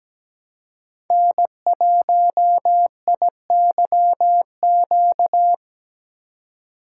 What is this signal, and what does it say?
Morse code, heard as a keyed tone, about 17 wpm, 700 Hz. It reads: N1IYQ
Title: USA Call Signs - 5 Characters - 2x Character Spacing - 17wpm